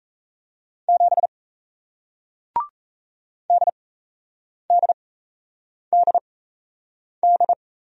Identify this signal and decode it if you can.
Morse code, heard as a keyed tone, about 38 wpm, spaced as for 8 wpm, 700 Hz. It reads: 7 DDDD